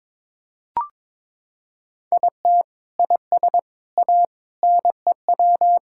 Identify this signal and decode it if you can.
Morse code, heard as a keyed tone, about 22 wpm, 700 Hz. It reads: IT IS A NEW
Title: Sentences from Top 1000 Words - Review Part 6 - 22wpm